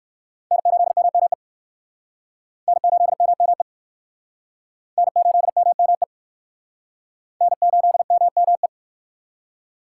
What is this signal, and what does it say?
Morse code, heard as a keyed tone, about 57 wpm, 700 Hz. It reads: N8MME N8MME N8MME N8MME